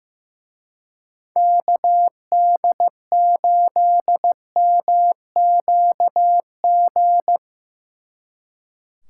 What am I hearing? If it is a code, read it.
Morse code, heard as a keyed tone, about 17 wpm, 700 Hz. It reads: KD8MQG
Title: Speed Racing - USA 2x3 Call Signs 15wpm